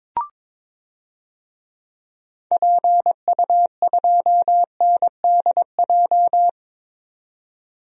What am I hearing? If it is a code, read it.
Morse code, heard as a keyed tone, about 22 wpm, 700 Hz. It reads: PU2NDJ
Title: DXCC-100 Call Signs - 6 Characters - 22wpm